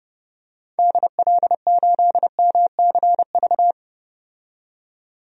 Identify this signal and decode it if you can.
Morse code, heard as a keyed tone, about 30 wpm, 700 Hz. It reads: DL8MCV
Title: DXCC-100 Call Signs - 6 Characters - 30wpm